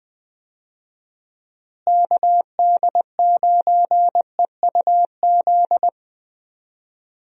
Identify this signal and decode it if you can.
Morse code, heard as a keyed tone, about 21 wpm, 700 Hz. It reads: KD9EUZ
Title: USA Call Signs - 6 Characters - Repeated 3x - 20wpm